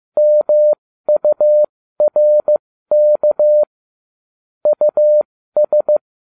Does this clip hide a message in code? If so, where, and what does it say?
Yes: Morse code from 0.2 to 6.0 s, heard as a keyed tone, about 15 wpm, spaced as for 13 wpm, 600 Hz. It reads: MURK US